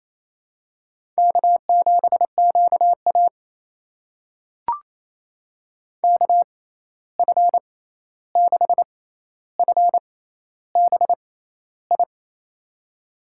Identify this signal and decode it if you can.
Morse code, heard as a keyed tone, about 28 wpm, spaced as for 13 wpm, 700 Hz. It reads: K7QA KF6FBI
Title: USA Call Signs - 6x Character Spacing - 28wpm